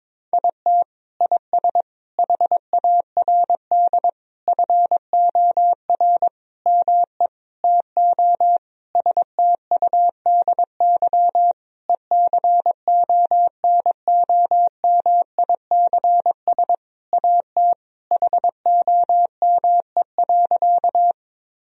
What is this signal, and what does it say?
Morse code, heard as a keyed tone, about 22 wpm, 700 Hz. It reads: IT IS HARD FOR ME TO STUDY ECONOMICS AT HOME.
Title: Sentences from Top 1000 Words - Review Part 3 - 22wpm